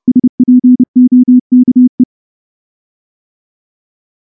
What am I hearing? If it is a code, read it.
Morse code, heard as a keyed tone, about 30 wpm, 250 Hz. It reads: SPOKE
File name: morse-training-30WPM